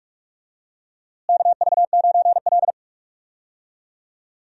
Morse code, heard as a keyed tone, about 45 wpm, 700 Hz. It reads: KV9L
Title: USA Call Signs - 4 Characters - Repeated 2x - 45wpm